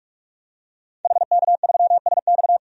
Morse code, heard as a keyed tone, about 46 wpm, 700 Hz. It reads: HK3SX
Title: DXCC-100 Call Signs - 5 Characters - 45wpm